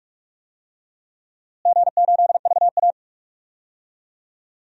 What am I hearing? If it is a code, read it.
Morse code, heard as a keyed tone, about 46 wpm, 700 Hz. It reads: G8VA
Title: DXCC-100 Call Signs - 4 Characters - Repeated 3x - 45wpm